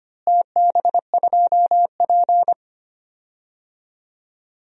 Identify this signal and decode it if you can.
Morse code, heard as a keyed tone, about 25 wpm, 700 Hz. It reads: TB2P
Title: DXCC-100 Call Signs - 4 Characters - Repeated 2x - 25wpm